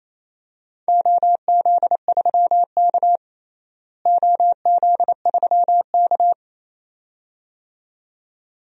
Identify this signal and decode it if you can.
Morse code, heard as a keyed tone, about 28 wpm, 700 Hz. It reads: OZ3K OZ3K
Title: DXCC-100 Call Signs - 4 Characters - Repeated 2x - 28wpm